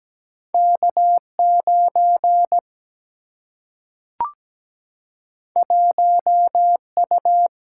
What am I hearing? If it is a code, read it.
Morse code, heard as a keyed tone, about 17 wpm, 700 Hz. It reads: K 9 1 U
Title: Single-Letter Single-Number 17wpm